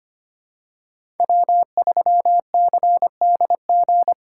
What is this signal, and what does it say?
Morse code, heard as a keyed tone, about 25 wpm, 700 Hz. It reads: W3CDG
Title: USA Call Signs - 5 Characters - 3x Character Spacing - 25wpm